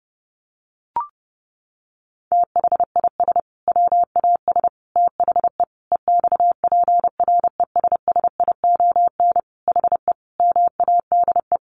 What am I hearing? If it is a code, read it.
Morse code, heard as a keyed tone, about 30 wpm, 700 Hz. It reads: THIS WAS THE EXPRESSION HE MADE
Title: Sentences from Top 1200 Words 30wpm